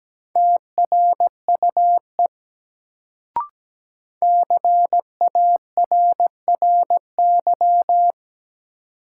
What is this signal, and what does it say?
Morse code, heard as a keyed tone, about 17 wpm, 700 Hz. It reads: TRUE CARRY
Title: Sentences from Top 400 Words - New Word Drill - 17wpm